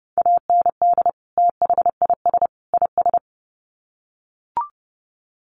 Morse code, heard as a keyed tone, about 30 wpm, 700 Hz. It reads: AND THIS IS
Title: Sets of 3 Words 30wpm